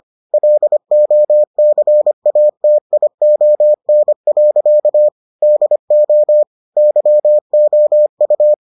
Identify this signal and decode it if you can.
Morse code, heard as a keyed tone, about 25 wpm, 600 Hz. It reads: LOCATION. DO YOU